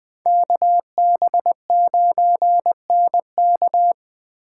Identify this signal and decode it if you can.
Morse code, heard as a keyed tone, about 20 wpm, 700 Hz. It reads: KB9NK
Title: Speed Racing - USA 2x2 Call Signs 17wpm